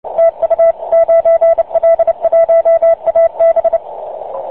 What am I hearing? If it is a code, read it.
Morse code, heard as a keyed tone, about 29 wpm, 660 Hz, with radio noise. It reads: TU 9L1AB
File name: 9l1ab-qso